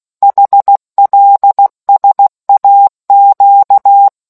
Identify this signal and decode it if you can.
Morse code, heard as a keyed tone, about 16 wpm, 800 Hz. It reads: HLSAQ